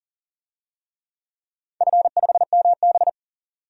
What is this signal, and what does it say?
Morse code, heard as a keyed tone, about 40 wpm, 700 Hz. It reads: F5MB
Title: DXCC-100 Call Signs - 4 Characters - 40wpm